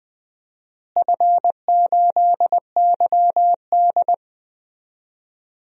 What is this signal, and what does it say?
Morse code, heard as a keyed tone, about 20 wpm, 700 Hz. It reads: F8YD
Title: DXCC-100 Call Signs - 4 Characters - Repeated 2x - 20wpm